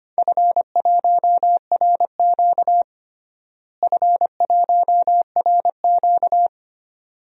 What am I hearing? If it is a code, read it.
Morse code, heard as a keyed tone, about 25 wpm, 700 Hz. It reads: F1RQ F1RQ
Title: DXCC-100 Call Signs - 4 Characters - Repeated 2x - 25wpm